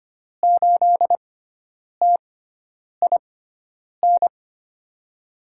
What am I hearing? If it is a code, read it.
Morse code, heard as a keyed tone, about 25 wpm, spaced as for 9 wpm, 700 Hz. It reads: 8TIN
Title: USA Call Signs - 6 Characters - 6x Character Spacing - 25wpm